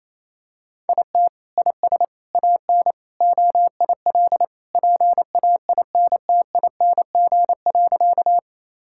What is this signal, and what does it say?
Morse code, heard as a keyed tone, about 28 wpm, 700 Hz. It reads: IT IS AN OIL PAINTING.
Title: Sentences from Top 1000 Words - Review Part 6 - 28wpm